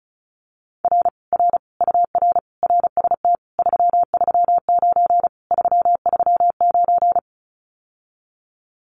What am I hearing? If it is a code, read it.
Morse code, heard as a keyed tone, about 36 wpm, 700 Hz. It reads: R R UR RST 339 339
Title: Common QSO Phrases Encore 35wpm